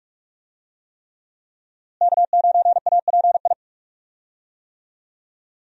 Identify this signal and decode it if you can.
Morse code, heard as a keyed tone, about 41 wpm, 700 Hz. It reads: K9API
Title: USA Call Signs - 5 Characters - Repeated 3x - 45wpm